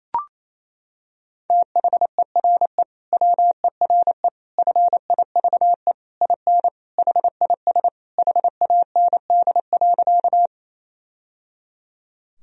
Morse code, heard as a keyed tone, about 27 wpm, 700 Hz. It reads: THERE WERE FIVE IN HIS HAND.
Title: Sentences from Top 1000 Words - Review Part 2 - 28wpm